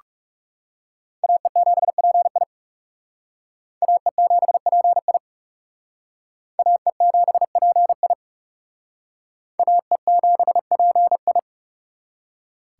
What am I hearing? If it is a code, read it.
Morse code, heard as a keyed tone, about 37 wpm, 700 Hz. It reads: AE7PI AE7PI AE7PI AE7PI
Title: Speed Racing - USA Call Signs 30wpm